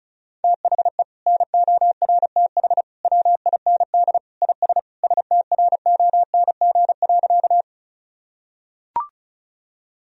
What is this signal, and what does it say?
Morse code, heard as a keyed tone, about 36 wpm, 700 Hz. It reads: THE NORTH WIND IS STRONG.
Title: Sentences from Top 1000 Words - Review Part 2 - 35wpm